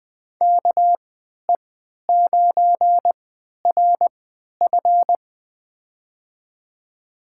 Morse code, heard as a keyed tone, about 20 wpm, spaced as for 11 wpm, 700 Hz. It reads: KE9RF
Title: USA Call Signs - 5 Characters - 3x Character Spacing - 20wpm